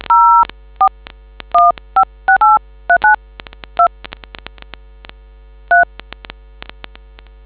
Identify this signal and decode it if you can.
DTMF, heard as telephone touch tones, tones of about 120 ms. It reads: 041568392 3